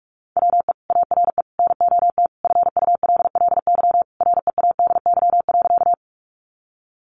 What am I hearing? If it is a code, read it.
Morse code, heard as a keyed tone, about 44 wpm, 700 Hz. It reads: WE ARE NOT FULLY READY.
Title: Sentences from Top 1300 Words 45wpm